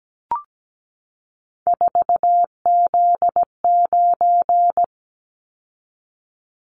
Morse code, heard as a keyed tone, about 17 wpm, 700 Hz. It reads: 4Z9
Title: Callsign Prefixes for Top 50 DXCC - 17wpm